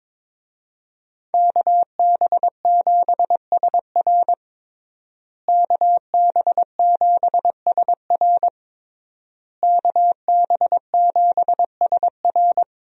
Morse code, heard as a keyed tone, about 22 wpm, 700 Hz. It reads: KB7SR KB7SR KB7SR
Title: USA Call Signs - 5 Characters - Repeated 3x - 22wpm